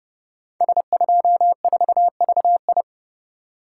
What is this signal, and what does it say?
Morse code, heard as a keyed tone, about 28 wpm, 700 Hz. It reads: S24VI
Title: Code Groups - 5 Letter:Numbers - 30wpm